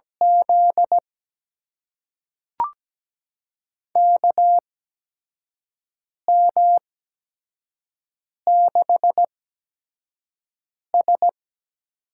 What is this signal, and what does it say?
Morse code, heard as a keyed tone, about 17 wpm, spaced as for 5 wpm, 700 Hz. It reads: Z KM6S
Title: USA Call Signs - 6 Characters - 8x Character Spacing - 17wpm